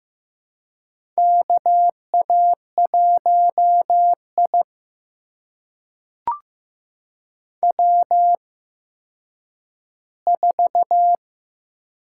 Morse code, heard as a keyed tone, about 15 wpm, spaced as for 9 wpm, 700 Hz. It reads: KA1I W4
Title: USA Call Signs - 4 Characters - 8x Character Spacing - 15wpm